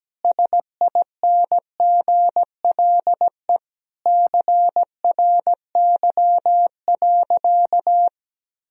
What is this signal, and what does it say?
Morse code, heard as a keyed tone, about 17 wpm, 700 Hz. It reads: SINGLE CRY.
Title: Sentences from Top 1000 Words - Review Part 4 - 17wpm